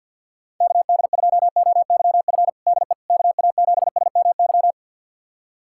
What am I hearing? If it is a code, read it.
Morse code, heard as a keyed tone, about 50 wpm, 700 Hz. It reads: KD2OYF DE KA7SMY
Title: USA Callsign DE Callsign - 6 Characters - 50wpm